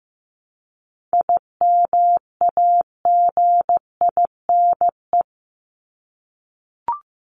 Morse code, heard as a keyed tone, about 15 wpm, 700 Hz. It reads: IMAGINE
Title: Top 1251-1500 Words 15wpm